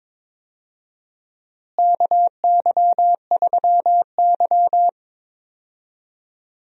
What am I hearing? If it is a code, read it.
Morse code, heard as a keyed tone, about 22 wpm, 700 Hz. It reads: KY3Y
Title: USA Call Signs - 4 Characters - Repeated 2x - 22wpm